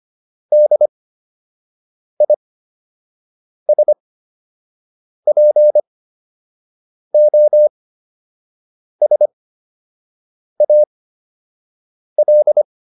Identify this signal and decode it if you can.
Morse code, heard as a keyed tone, about 25 wpm, spaced as for 6 wpm, 600 Hz. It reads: DISPOSAL